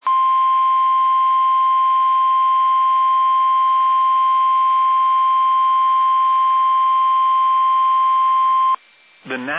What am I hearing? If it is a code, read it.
Emergency Alert System (SAME) tones (the 1050 Hz attention tone), with speech. It is heard from 0.0 to 8.8 s.